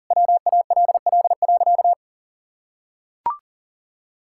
Morse code, heard as a keyed tone, about 40 wpm, 700 Hz. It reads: WALL.
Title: Sentences from Top 1600 Words 40wpm